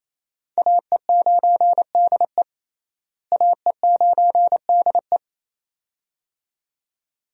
Morse code, heard as a keyed tone, about 27 wpm, 700 Hz. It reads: AE9DE AE9DE
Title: USA Call Signs - 5 Characters - Repeated 2x - 28wpm